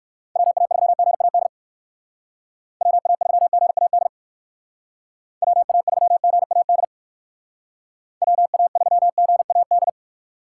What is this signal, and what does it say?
Morse code, heard as a keyed tone, about 54 wpm, 700 Hz. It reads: WA3GAD WA3GAD WA3GAD WA3GAD
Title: Speed Racing - USA 2x3 Call Signs 45wpm